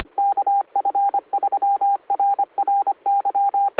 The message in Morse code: KF3RRY